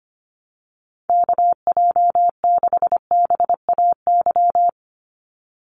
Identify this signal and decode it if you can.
Morse code, heard as a keyed tone, about 25 wpm, 700 Hz. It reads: KJ6BAY
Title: USA Call Signs 25wpm